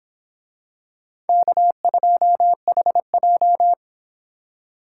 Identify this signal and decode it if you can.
Morse code, heard as a keyed tone, about 26 wpm, 700 Hz. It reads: K2HJ